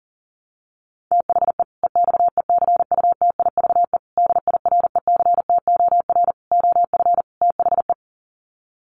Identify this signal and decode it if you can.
Morse code, heard as a keyed tone, about 40 wpm, 700 Hz. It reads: THE EXECUTIVE DIRECTOR OF THE
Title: Sets of 5 Words Encore 40wpm